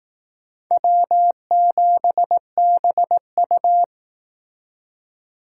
Morse code, heard as a keyed tone, about 19 wpm, 700 Hz. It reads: W7BU
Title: Speed Racing - USA 1x2 Call Signs 15wpm